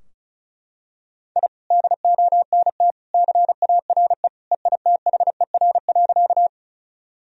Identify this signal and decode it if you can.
Morse code, heard as a keyed tone, about 34 wpm, 700 Hz. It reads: I DONT CARE EITHER.
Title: Sentences from Top 1000 Words - Review Part 4 - 35wpm